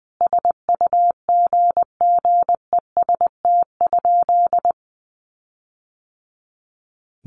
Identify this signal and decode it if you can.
Morse code, heard as a keyed tone, about 20 wpm, 700 Hz. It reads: SUGGEST?
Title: Sentences from Top 1300 Words 20wpm